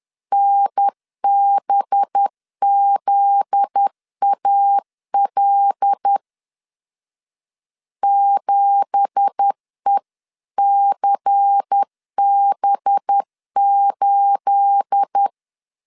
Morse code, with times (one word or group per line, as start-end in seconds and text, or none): 0.3-6.2 NBZAL
8.0-15.3 7ECB8